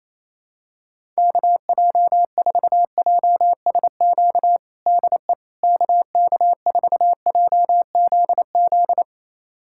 Morse code, heard as a keyed tone, about 28 wpm, 700 Hz. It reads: KJ4JSQ DE KK4JZZ
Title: USA Callsign DE Callsign 28wpm